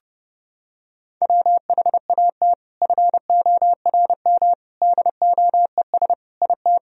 Morse code, heard as a keyed tone, about 30 wpm, 700 Hz. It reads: WHAT FORM DOES IT
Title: Sentences from Top 1000 Words - Review Part 1 - 30wpm